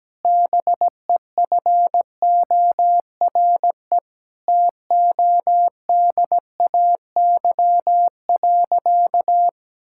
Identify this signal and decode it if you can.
Morse code, heard as a keyed tone, about 17 wpm, 700 Hz. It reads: BEFORE TODAY.
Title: Sentences from Top 1000 Words - Review Part 6 - 17wpm